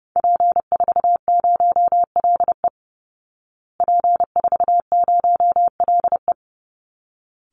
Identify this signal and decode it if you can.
Morse code, heard as a keyed tone, about 30 wpm, 700 Hz. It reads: P40LE P40LE
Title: Call Signs Repeated 3x - 30wpm